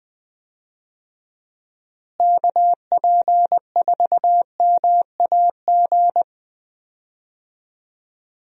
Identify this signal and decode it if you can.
Morse code, heard as a keyed tone, about 20 wpm, 700 Hz. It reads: KP4MAG
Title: USA Call Signs - 6 Characters - Repeated 3x - 20wpm